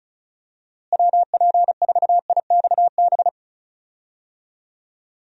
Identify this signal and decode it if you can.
Morse code, heard as a keyed tone, about 34 wpm, 700 Hz. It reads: WP4IXB
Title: USA Call Signs - 6 Characters - Repeated 2x - 35wpm